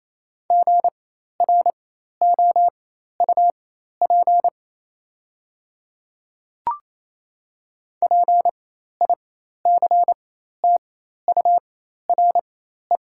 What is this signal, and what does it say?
Morse code, heard as a keyed tone, about 28 wpm, spaced as for 13 wpm, 690 Hz. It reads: GROUP PICTURE
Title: 4x Character Spacing - Top 300 Words - 28wpm